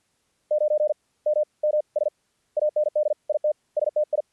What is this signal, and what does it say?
Morse code, heard as a keyed tone, about 50 wpm, spaced as for 37 wpm, 600 Hz. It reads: 9MMS AND IT STI
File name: archived / CRSnet